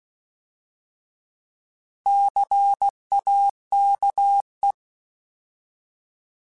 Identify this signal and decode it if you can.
Morse code, heard as a keyed tone, about 16 wpm, 800 Hz. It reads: CAKE
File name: Alphabetical / Alphabetical mp3s / Cs @15